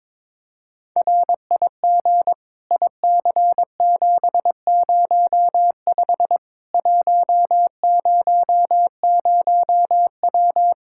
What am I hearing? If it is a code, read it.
Morse code, heard as a keyed tone, about 22 wpm, 700 Hz. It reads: RIG IC705 100W
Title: K1USN SST QSO Practice 22wpm